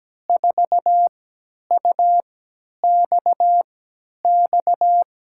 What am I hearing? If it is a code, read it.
Morse code, heard as a keyed tone, about 17 wpm, spaced as for 10 wpm, 700 Hz. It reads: 4UXX
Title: USA Call Signs - 5 Characters - 3x Character Spacing - 17wpm